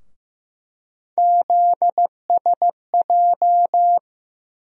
Morse code, heard as a keyed tone, about 15 wpm, 700 Hz. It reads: ZSJ
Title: USA Call Sign Postfixes - 3 Letters 15wpm